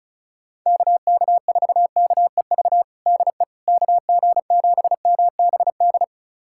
Morse code, heard as a keyed tone, about 36 wpm, 700 Hz. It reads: KK4KEV DE KG7MBD
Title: USA Callsign DE Callsign - 6 Characters - 35wpm